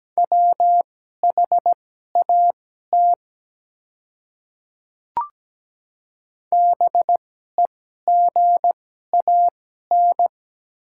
Morse code, heard as a keyed tone, about 17 wpm, spaced as for 12 wpm, 700 Hz. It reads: WHAT BEGAN
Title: 2x Character Spacing - Top 300 Words - 17wpm